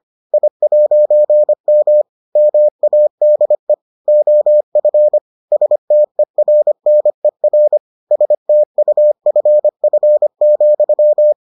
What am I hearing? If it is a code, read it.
Morse code, heard as a keyed tone, about 25 wpm, 600 Hz. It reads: I'M MADE OF STERNER STUFF,